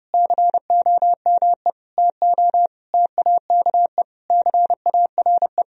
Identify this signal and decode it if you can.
Morse code, heard as a keyed tone, about 30 wpm, 700 Hz. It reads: COME TO TAKE CARE
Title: Sentences from Top 1000 Words - Review Part 1 - 30wpm